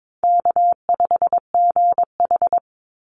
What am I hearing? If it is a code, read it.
Morse code, heard as a keyed tone, about 22 wpm, 690 Hz. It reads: K5GH
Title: Easy Call Signs Repeated 2x - 22wpm